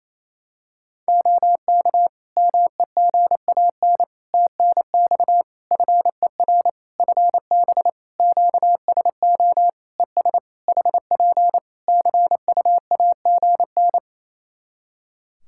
Morse code, heard as a keyed tone, about 28 wpm, 700 Hz. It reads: OK MEGAN TNX FER FB QSO ES HP CUAGN
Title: Practice QSO - Exchange 4 - Ending - 28wpm